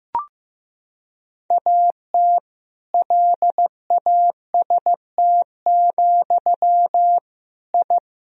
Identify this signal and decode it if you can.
Morse code, heard as a keyed tone, about 16 wpm, 700 Hz. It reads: AT LAST, I
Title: Sentences from Top 1000 Words - Review Part 1 - 15wpm